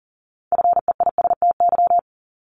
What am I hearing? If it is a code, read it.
Morse code, heard as a keyed tone, about 40 wpm, 700 Hz. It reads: FEISTY
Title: Multi-letter Phonemes Repeated 3x 40wpm